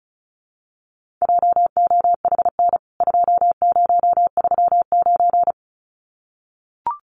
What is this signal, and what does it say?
Morse code, heard as a keyed tone, about 34 wpm, 700 Hz. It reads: JOHN 2039
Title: CWT Exchange 35wpm